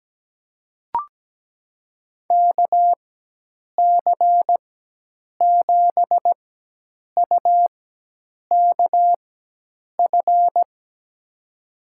Morse code, heard as a keyed tone, about 17 wpm, spaced as for 8 wpm, 700 Hz. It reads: KC7UKF
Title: USA Call Signs - 6 Characters - 4x Character Spacing - 17wpm